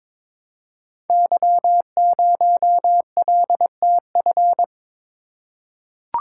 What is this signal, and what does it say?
Morse code, heard as a keyed tone, about 22 wpm, 700 Hz. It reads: Y0LTF
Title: Code Groups - 5 Letter:Numbers - 22wpm